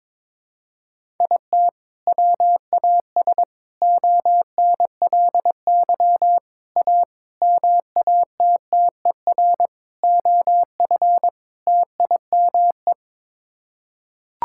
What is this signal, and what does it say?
Morse code, heard as a keyed tone, about 22 wpm, 700 Hz. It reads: IT WAS ONLY A MATTER OF TIME
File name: practice / 22wpm / Sets of 7 Words - 22wpm